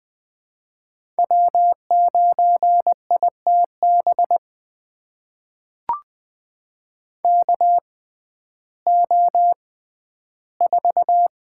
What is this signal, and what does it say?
Morse code, heard as a keyed tone, about 20 wpm, spaced as for 12 wpm, 700 Hz. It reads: W9ITB KO4
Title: USA Call Signs - 6x Character Spacing - 20wpm